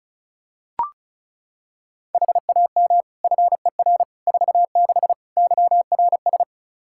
Morse code, heard as a keyed tone, about 36 wpm, 700 Hz. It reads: HAM FER 46 YRS
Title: Practice QSO - Exchange 3 - More Info - 35wpm